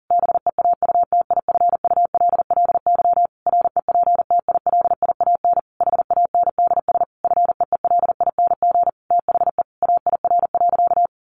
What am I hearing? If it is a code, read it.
Morse code, heard as a keyed tone, about 40 wpm, 700 Hz. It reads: BEAUTIFULLY REPTILIAN HANDS FEELING THE AIR.
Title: A Sound Of Thunder 40wpm